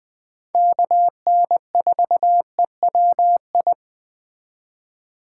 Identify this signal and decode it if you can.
Morse code, heard as a keyed tone, about 21 wpm, 700 Hz. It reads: KN4EWI
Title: K1USN SST QSO Practice - 20wpm